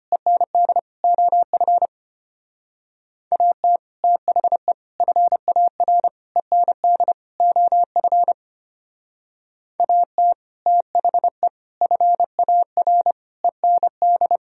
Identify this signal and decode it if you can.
Morse code, heard as a keyed tone, about 31 wpm, 700 Hz. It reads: END OF AT THE FAR END OF AT THE FAR END
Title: Speed-Racing - Sets of 5 Words 25wpm